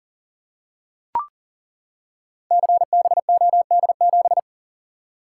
Morse code, heard as a keyed tone, about 38 wpm, 700 Hz. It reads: CBOD7
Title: Code Groups - 5 Letter:Numbers - 40wpm